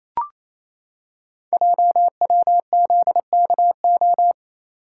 Morse code, heard as a keyed tone, about 27 wpm, 700 Hz. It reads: JWZKO
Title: Code Groups - 5 Letters - 28wpm